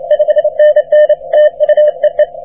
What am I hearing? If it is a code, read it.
Morse code, heard as a keyed tone, about 26 wpm, 600 Hz. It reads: 5NN TU EE
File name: wp / k4m 40m cw 17 oct 2009 0729z